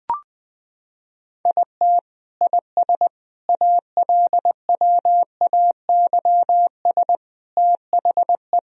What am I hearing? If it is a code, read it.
Morse code, heard as a keyed tone, about 19 wpm, 700 Hz. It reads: IT IS ALWAYS THE
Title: Sentences from Top 1000 Words - Review Part 1 - 20wpm